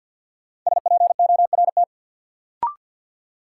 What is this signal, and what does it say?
Morse code, heard as a keyed tone, about 52 wpm, 700 Hz. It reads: SPORT
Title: Sentences from Top 1500 Words - New Word Drill Alt - 50wpm